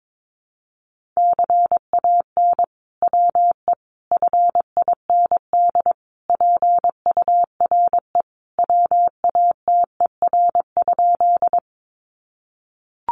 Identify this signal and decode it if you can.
Morse code, heard as a keyed tone, about 22 wpm, 700 Hz. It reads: CAN WE FIND PURE WATER?
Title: Sentences from Top 1500 Words 22wpm